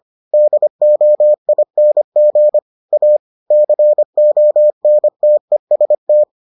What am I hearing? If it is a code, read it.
Morse code, heard as a keyed tone, about 25 wpm, 600 Hz. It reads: DOING A CONTEST